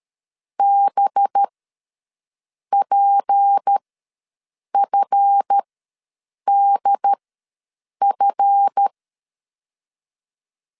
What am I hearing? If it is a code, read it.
Morse code, heard as a keyed tone, about 13 wpm, spaced as for 6 wpm, 800 Hz. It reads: BPFDF
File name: LEZIONI / 7 GF9Z / 7 GF ws 12wpm 3140